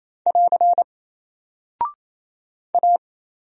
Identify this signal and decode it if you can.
Morse code, heard as a keyed tone, about 27 wpm, 700 Hz. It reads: <AR> A
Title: Single Character (Letter:Number:Punctuation:Prosign) 28wpm